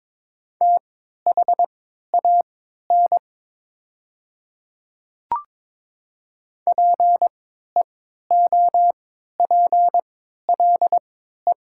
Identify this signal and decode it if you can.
Morse code, heard as a keyed tone, about 22 wpm, spaced as for 12 wpm, 700 Hz. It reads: THAN PEOPLE
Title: 3x Character Spacing - Top 100 Words - 22wpm